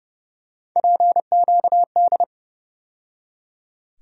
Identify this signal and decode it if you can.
Morse code, heard as a keyed tone, about 32 wpm, 700 Hz. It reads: PQD
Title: USA Call Sign Postfixes - 3 Letters 30wpm